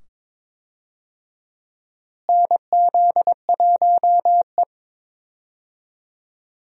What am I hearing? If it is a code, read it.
Morse code, heard as a keyed tone, about 22 wpm, 700 Hz. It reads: NZ1E
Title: USA Call Signs - 4 Characters - Repeated 2x - 22wpm